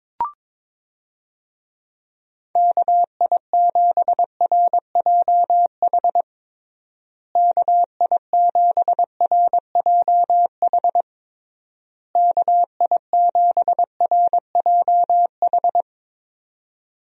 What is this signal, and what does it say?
Morse code, heard as a keyed tone, about 22 wpm, 700 Hz. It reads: KI7RJH KI7RJH KI7RJH
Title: USA Call Signs - 6 Characters - Repeated 3x - 22wpm